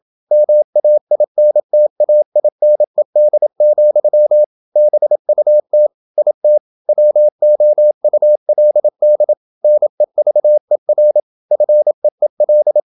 Morse code, heard as a keyed tone, about 28 wpm, 600 Hz. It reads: MAINTAINED, BUT IT WOULD NEVER FEEL
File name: Never 0027wpm